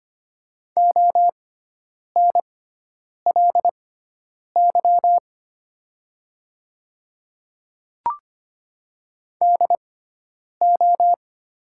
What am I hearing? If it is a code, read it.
Morse code, heard as a keyed tone, about 25 wpm, spaced as for 9 wpm, 690 Hz. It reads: ONLY DO
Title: 6x Character Spacing - Top 200 Words - 25wpm